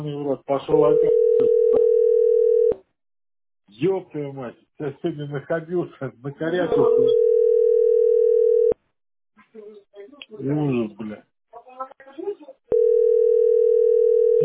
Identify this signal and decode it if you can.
Ringback tone (North America), 440 + 480 Hz, with speech.
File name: 893f47c0-11a8-4cf1-ab3f